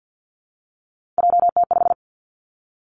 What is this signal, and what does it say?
Morse code, heard as a keyed tone, about 50 wpm, 700 Hz. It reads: JT5
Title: Three Challenging Letter-Number Combinations 50wpm